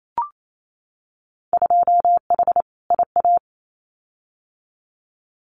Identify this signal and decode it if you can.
Morse code, heard as a keyed tone, about 28 wpm, 700 Hz. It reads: 2H IA
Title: Winter Field Day Exchanges Encore 28wpm